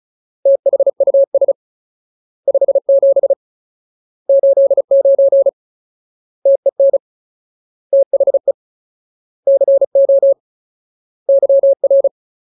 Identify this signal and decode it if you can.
Morse code, heard as a keyed tone, about 34 wpm, 550 Hz. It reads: THUS 57 89 TEN THE CO YR
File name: Ultimate Word List / 35WPM Ultimate